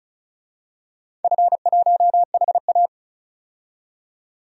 Morse code, heard as a keyed tone, about 36 wpm, 700 Hz. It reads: F1HA
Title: DXCC-100 Call Signs - 4 Characters - 35wpm